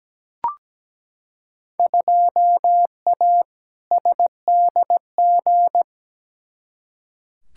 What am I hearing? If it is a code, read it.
Morse code, heard as a keyed tone, about 17 wpm, 700 Hz. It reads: 2A SDG
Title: Field Day Exchanges Encore - 17wpm